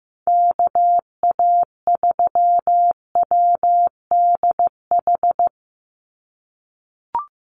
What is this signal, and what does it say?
Morse code, heard as a keyed tone, about 15 wpm, 700 Hz. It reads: KA3WDH
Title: USA Call Signs 15wpm